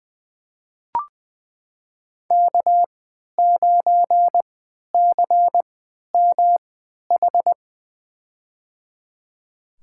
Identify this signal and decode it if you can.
Morse code, heard as a keyed tone, about 20 wpm, spaced as for 11 wpm, 700 Hz. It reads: K9CMH
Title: USA Call Signs - 5 Characters - 3x Character Spacing - 20wpm